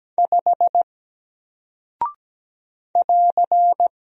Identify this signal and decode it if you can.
Morse code, heard as a keyed tone, about 17 wpm, 700 Hz. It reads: 5 <AR>
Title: Single Character (Letter:Number:Punctuation:Prosign) 17wpm